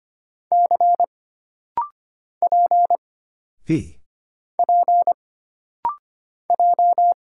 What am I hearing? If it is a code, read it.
Morse code, heard as a keyed tone, about 25 wpm, 700 Hz. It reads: C P P J
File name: Single Character (Letter:Number:Punctuation:Prosign) - Rapid-Fire - 25wpm